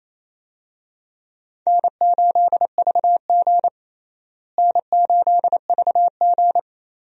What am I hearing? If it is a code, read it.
Morse code, heard as a keyed tone, about 28 wpm, 700 Hz. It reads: N8VG N8VG
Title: USA Call Signs - 4 Characters - Repeated 2x - 28wpm